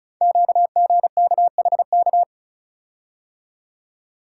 Morse code, heard as a keyed tone, about 36 wpm, 690 Hz. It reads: QGKHK
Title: Code Groups - 5 Letters - 35wpm